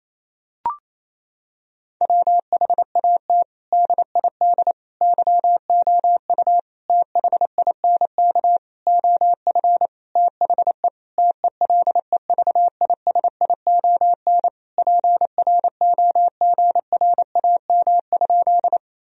Morse code, heard as a keyed tone, about 28 wpm, 700 Hz. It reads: WHAT DID YOU THINK OF THE TELEVISION PROGRAM?
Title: Sentences from Top 1000 Words - Review Part 4 - 28wpm